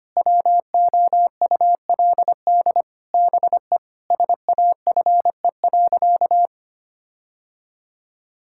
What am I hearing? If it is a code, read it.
Morse code, heard as a keyed tone, about 25 wpm, 700 Hz. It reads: WOULD BE SAFE.
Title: Sentences from Top 1600 Words 25wpm